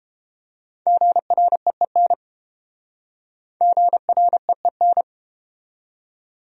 Morse code, heard as a keyed tone, about 31 wpm, 700 Hz. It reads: GREEN GREEN